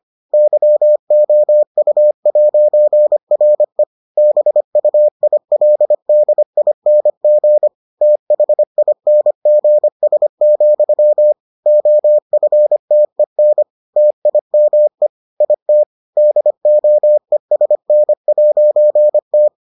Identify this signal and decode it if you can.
Morse code, heard as a keyed tone, about 25 wpm, 600 Hz. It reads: YOU'RE BUILDING THINGS, OFTEN TIME IT DOESN'T